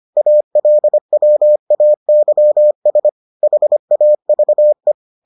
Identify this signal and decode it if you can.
Morse code, heard as a keyed tone, about 25 wpm, 600 Hz. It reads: ALWAYS HAVE